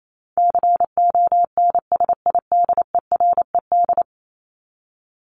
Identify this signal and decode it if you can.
Morse code, heard as a keyed tone, about 28 wpm, 700 Hz. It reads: CONSIDERED
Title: ED Suffix - Top 100 Words with Base Words 28wpm